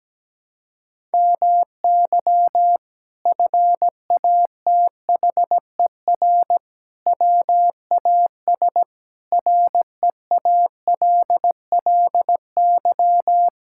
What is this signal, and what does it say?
Morse code, heard as a keyed tone, about 17 wpm, 700 Hz. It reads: MY FATHER WAS REALLY
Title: Sentences from Top 1000 Words - Review Part 6 - 17wpm